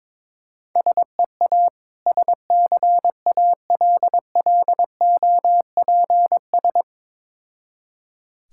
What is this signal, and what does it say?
Morse code, heard as a keyed tone, about 22 wpm, 690 Hz. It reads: SEA SCALLOPS
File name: Difficult Words with Hints - 22wpm